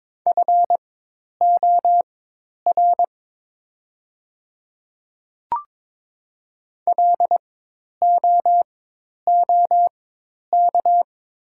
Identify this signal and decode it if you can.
Morse code, heard as a keyed tone, about 23 wpm, spaced as for 10 wpm, 700 Hz. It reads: FOR LOOK
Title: 4x Character Spacing - Top 100 Words - 22wpm